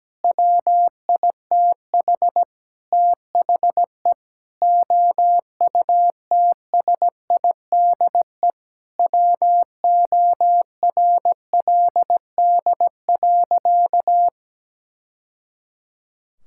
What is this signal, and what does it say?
Morse code, heard as a keyed tone, about 17 wpm, 700 Hz. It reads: WITH THE OUTSIDE WORLD.
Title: Sentences from Top 1000 Words - Review Part 6 - 17wpm